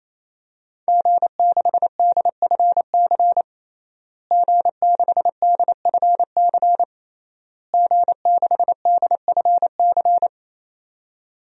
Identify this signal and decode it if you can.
Morse code, heard as a keyed tone, about 28 wpm, 700 Hz. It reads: G6DFC G6DFC G6DFC
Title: DXCC-100 Call Signs - 5 Characters - Repeated 3x - 28wpm